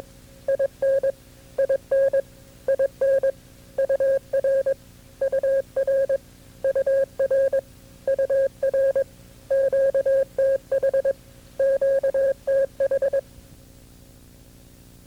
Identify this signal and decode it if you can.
Morse code, heard as a keyed tone, about 25 wpm, 540 Hz. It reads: IN IN IN UR UR UR UR QTH QTH
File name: CW-practice/KB6MT-tape2-side1